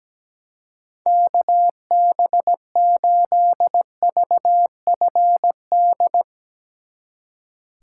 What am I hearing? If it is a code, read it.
Morse code, heard as a keyed tone, about 17 wpm, 700 Hz. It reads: KB8VFD
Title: Speed Racing - USA 2x3 Call Signs 17wpm